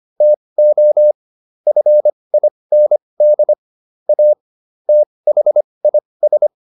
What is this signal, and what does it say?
Morse code, heard as a keyed tone, about 25 wpm, spaced as for 20 wpm, 600 Hz. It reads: TO FIND A THIS